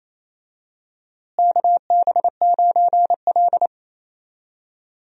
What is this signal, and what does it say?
Morse code, heard as a keyed tone, about 28 wpm, 700 Hz. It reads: KB9L
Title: USA Call Signs - 4 Characters - Repeated 2x - 28wpm